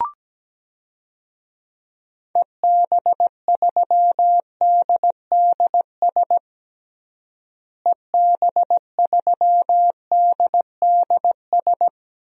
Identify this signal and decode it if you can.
Morse code, heard as a keyed tone, about 17 wpm, 700 Hz. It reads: EB3DDS EB3DDS
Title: DXCC-100 Call Signs - 6 Characters - Repeated 2x - 17wpm